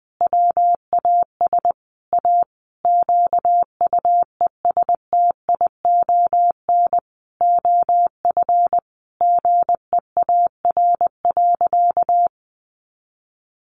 Morse code, heard as a keyed tone, about 20 wpm, 700 Hz. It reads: WAS A QUESTION OF GEAR.
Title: Sentences from Top 1400 Words 20wpm